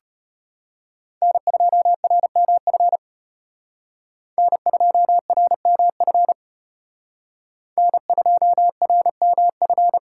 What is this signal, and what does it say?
Morse code, heard as a keyed tone, about 32 wpm, 700 Hz. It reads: N2RMF N2RMF N2RMF